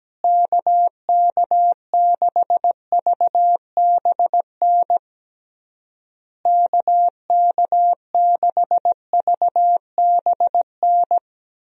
Morse code, heard as a keyed tone, about 17 wpm, 700 Hz. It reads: KK6VBN KK6VBN
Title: USA Call Signs - 6 Characters - Repeated 3x - 17wpm